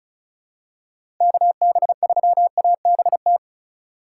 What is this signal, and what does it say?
Morse code, heard as a keyed tone, about 36 wpm, 700 Hz. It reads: KB3ABT
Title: USA Call Signs - 4x Character Spacing - 35wpm